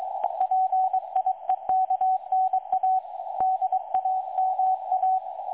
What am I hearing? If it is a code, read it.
Morse code, heard as a keyed tone, about 21 wpm, 750 Hz. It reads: PIEKNA DAMA
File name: dlazaawansowanych / nagraniazpasma / websdr recording 3565.0kHz